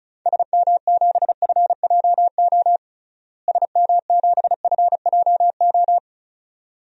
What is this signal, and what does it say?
Morse code, heard as a keyed tone, about 36 wpm, 700 Hz. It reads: SM7FJO SM7FJO
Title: DXCC-100 Call Signs - 6 Characters - Repeated 2x - 35wpm